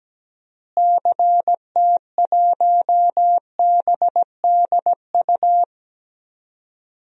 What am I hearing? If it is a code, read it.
Morse code, heard as a keyed tone, about 17 wpm, 700 Hz. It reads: CT1BDU
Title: DXCC-100 Call Signs - 6 Characters - Repeated 2x - 17wpm